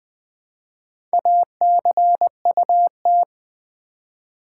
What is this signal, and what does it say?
Morse code, heard as a keyed tone, about 20 wpm, 700 Hz. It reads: ACUT
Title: Build A Word - 1 Letter At A Time - 5-Letter Words - 20wpm